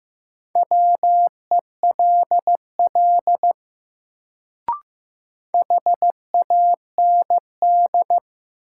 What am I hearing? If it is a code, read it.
Morse code, heard as a keyed tone, about 15 wpm, 700 Hz. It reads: WELL HAND
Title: Sentences from Top 200 Words - New Word Drill - 15wpm